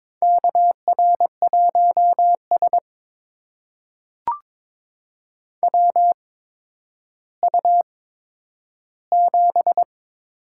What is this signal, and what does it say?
Morse code, heard as a keyed tone, about 21 wpm, spaced as for 11 wpm, 700 Hz. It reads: KR1S WU7
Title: USA Call Signs - 4 Characters - 8x Character Spacing - 22wpm